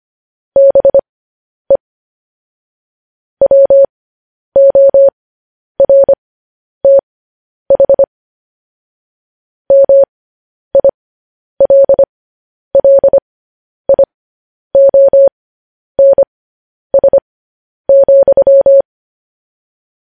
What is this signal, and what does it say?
Morse code, heard as a keyed tone, about 25 wpm, spaced as for 10 wpm, 550 Hz. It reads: BE WORTH MILLIONS,
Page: news-headlines-in-morse-code-at-10-wpm